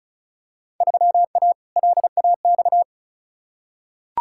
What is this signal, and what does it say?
Morse code, heard as a keyed tone, about 36 wpm, 700 Hz. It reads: 3A LAX
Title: Field Day Exchanges - 35wpm